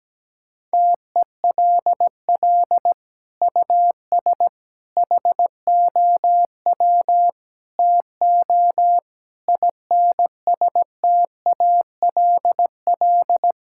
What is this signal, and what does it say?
Morse code, heard as a keyed tone, about 17 wpm, 700 Hz. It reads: TELL US HOW TO INSTALL
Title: Sentences from Top 1600 Words 17wpm